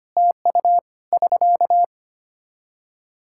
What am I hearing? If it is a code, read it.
Morse code, heard as a keyed tone, about 25 wpm, 700 Hz. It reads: TU <SK>
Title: Practice QSO - Exchange 4 - Ending - 25wpm